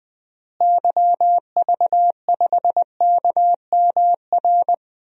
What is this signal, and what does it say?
Morse code, heard as a keyed tone, about 20 wpm, 700 Hz. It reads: YV5KMR
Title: DXCC-100 Call Signs - 6 Characters - Repeated 3x - 20wpm